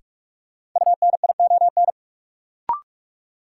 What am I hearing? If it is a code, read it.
Morse code, heard as a keyed tone, about 46 wpm, 700 Hz. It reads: UNION
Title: Sentences from Top 1400 Words - New Word Drill - 45wpm